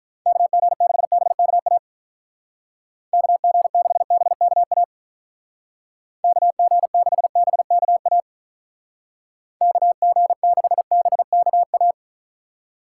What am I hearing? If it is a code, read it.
Morse code, heard as a keyed tone, about 43 wpm, 700 Hz. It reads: KG6BKA KG6BKA KG6BKA KG6BKA
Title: Speed Racing - USA 2x3 Call Signs 35wpm